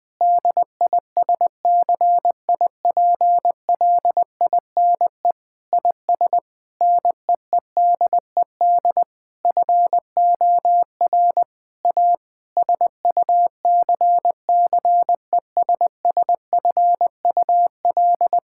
Morse code, heard as a keyed tone, about 21 wpm, 700 Hz. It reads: DISCIPLINE IS NEEDED FOR A SUCCESSFUL
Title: Sentences from Top 1000 Words - Review Part 6 - 20wpm